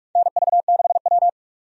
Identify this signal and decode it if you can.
Morse code, heard as a keyed tone, about 46 wpm, 700 Hz. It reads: NV6W
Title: USA Call Signs - 4 Characters - 45wpm